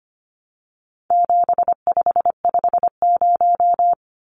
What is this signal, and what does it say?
Morse code, heard as a keyed tone, about 25 wpm, 700 Hz. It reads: 7550
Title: Numbers - 4-Digit 25wpm